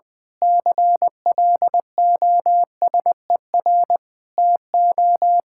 Morse code, heard as a keyed tone, about 20 wpm, 700 Hz. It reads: CLOSER TO
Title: Pangram Challenge v2 - 20wpm